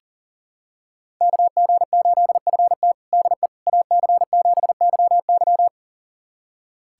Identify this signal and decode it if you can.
Morse code, heard as a keyed tone, about 40 wpm, 700 Hz. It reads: KG8FT DE AC7YY
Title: USA Callsign DE Callsign - 5 Characters - 40wpm